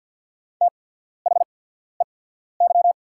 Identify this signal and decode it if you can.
Morse code, heard as a keyed tone, about 48 wpm, spaced as for 14 wpm, 690 Hz. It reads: THEY
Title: 8x Character Spacing - Top 100 Words - 50wpm